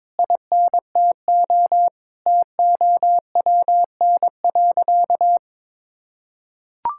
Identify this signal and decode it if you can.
Morse code, heard as a keyed tone, about 22 wpm, 700 Hz. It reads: INTO TOWN.
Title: Sentences from Top 1000 Words - Review Part 6 - 22wpm